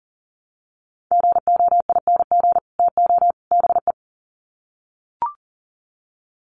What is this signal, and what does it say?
Morse code, heard as a keyed tone, about 40 wpm, 700 Hz. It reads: GOING TO BE
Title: Sets of 3 Words 40wpm